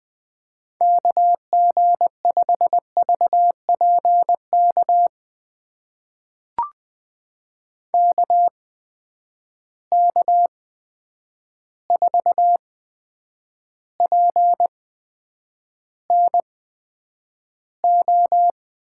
Morse code, heard as a keyed tone, about 21 wpm, spaced as for 9 wpm, 700 Hz. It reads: KG5VPK KK4PNO
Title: USA Call Signs - 6 Characters - 8x Character Spacing - 20wpm